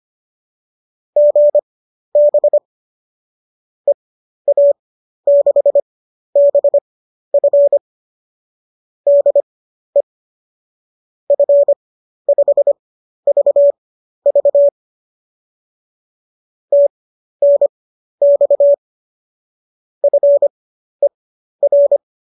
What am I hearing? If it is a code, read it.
Morse code, heard as a keyed tone, about 25 wpm, spaced as for 12 wpm, 570 Hz. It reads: GB EA6BF DE F5VV TNX FER
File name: qso009 12